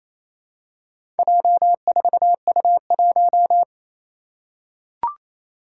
Morse code, heard as a keyed tone, about 27 wpm, 700 Hz. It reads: J4U1
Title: Four Challenging Letter-Number Combinations 28wpm